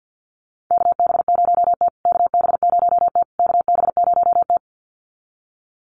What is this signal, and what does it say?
Morse code, heard as a keyed tone, about 50 wpm, 700 Hz. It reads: KB0T KB0T KB0T
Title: USA Call Signs Repeated 3x 50wpm